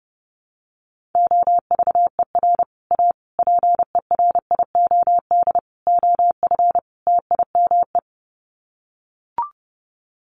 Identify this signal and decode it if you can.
Morse code, heard as a keyed tone, about 30 wpm, 700 Hz. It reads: OVER A PERIOD OF TIME
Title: Sets of 5 Words Encore2 30wpm